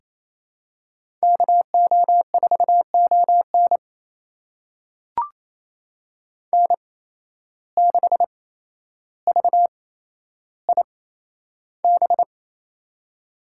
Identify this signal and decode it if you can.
Morse code, heard as a keyed tone, about 28 wpm, spaced as for 12 wpm, 700 Hz. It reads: KO4ON N6VIB
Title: USA Call Signs - 5 Characters - 8x Character Spacing - 28wpm